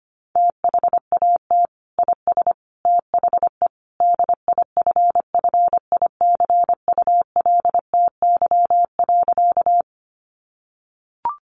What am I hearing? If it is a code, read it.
Morse code, heard as a keyed tone, about 25 wpm, 700 Hz. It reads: THAT IS THE DIFFICULTY.
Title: Sentences from Top 1500 Words 25wpm